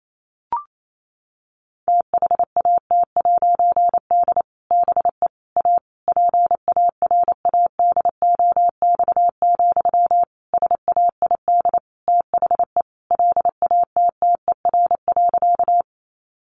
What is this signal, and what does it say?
Morse code, heard as a keyed tone, about 28 wpm, 700 Hz. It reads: THAT'D BE A PARADOX, SAID THE LATTER.
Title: A Sound Of Thunder 28wpm